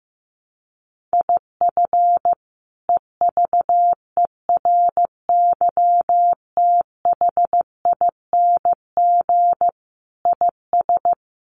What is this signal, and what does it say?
Morse code, heard as a keyed tone, about 15 wpm, 700 Hz. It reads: IF EVERYTHING IS
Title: Sentences from Top 700 Words 15wpm